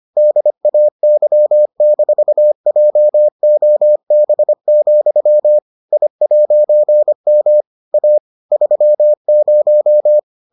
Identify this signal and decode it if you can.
Morse code, heard as a keyed tone, about 25 wpm, 600 Hz. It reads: DAY-JOB, I'M A 30